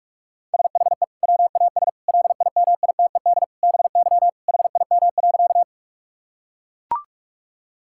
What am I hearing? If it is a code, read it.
Morse code, heard as a keyed tone, about 43 wpm, 700 Hz. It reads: SHE WAS LIMITED BY HIM.
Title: Sentences from Top 1000 Words - Review Part 6 - 45wpm